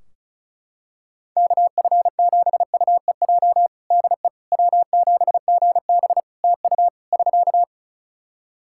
Morse code, heard as a keyed tone, about 36 wpm, 700 Hz. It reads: KF7UEJ DE W7GB TU <SK>
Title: Practice QSO - Exchange 4 - Ending - 35wpm